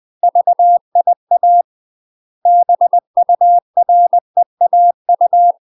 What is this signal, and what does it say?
Morse code, heard as a keyed tone, about 20 wpm, 700 Hz. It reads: VIA BUREAU
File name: UK MASTER QSOs TRAINING / 20WPM / UK MASTER QSOs 0036